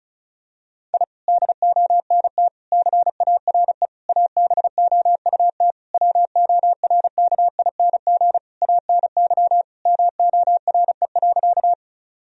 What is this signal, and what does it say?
Morse code, heard as a keyed tone, about 34 wpm, 700 Hz. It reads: I DONT CARE ABOUT WORKING ANY MORE.
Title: Sentences from Top 1000 Words - Review Part 4 - 35wpm